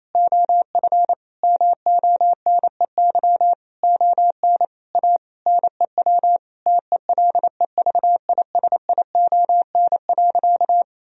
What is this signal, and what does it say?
Morse code, heard as a keyed tone, about 28 wpm, 700 Hz. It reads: OF MONEY ON A NEW TELEVISION.
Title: Sentences from Top 1000 Words - Review Part 4 - 28wpm